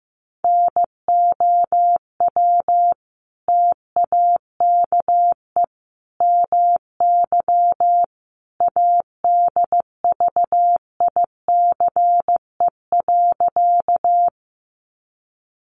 Morse code, heard as a keyed tone, about 15 wpm, 700 Hz. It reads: NOW TAKE MY ADVICE.
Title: Sentences from Top 1100 Words 15wpm